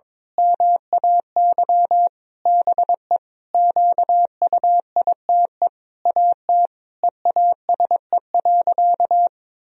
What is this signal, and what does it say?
Morse code, heard as a keyed tone, about 22 wpm, 700 Hz. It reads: MAY BE QUITE AT EASE.
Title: Sentences from Top 1000 Words - Review Part 1 - 22wpm